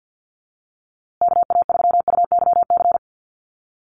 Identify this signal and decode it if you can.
Morse code, heard as a keyed tone, about 50 wpm, 700 Hz. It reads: KA3UYC
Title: USA Call Signs Repeated 2x 50wpm